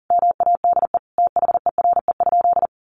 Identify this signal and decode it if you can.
Morse code, heard as a keyed tone, about 40 wpm, 700 Hz. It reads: MADE THERE?
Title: Sentences from Top 1400 Words 40wpm